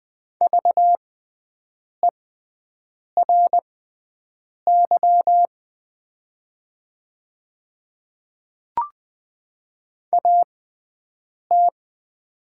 Morse code, heard as a keyed tone, about 20 wpm, spaced as for 7 wpm, 690 Hz. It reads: VERY AT
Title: 6x Character Spacing - Top 300 Words - 20wpm